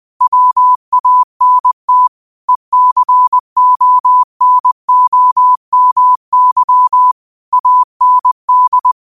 Morse code, heard as a keyed tone, about 20 wpm, 1000 Hz. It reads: WANT ECONOMY AND